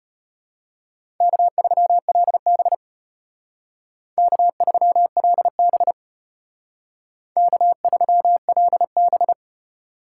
Morse code, heard as a keyed tone, about 33 wpm, 700 Hz. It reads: K3LB K3LB K3LB